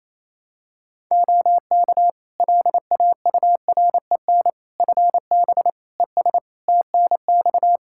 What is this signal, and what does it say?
Morse code, heard as a keyed tone, about 28 wpm, 700 Hz. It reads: OK LAUREN FB ES TNX
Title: Practice QSO - Exchange 2 - Info - 28wpm